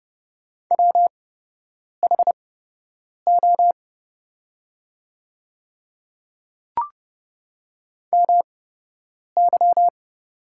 Morse code, heard as a keyed tone, about 30 wpm, spaced as for 8 wpm, 700 Hz. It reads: WHO MY
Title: 8x Character Spacing - Top 100 Words - 30wpm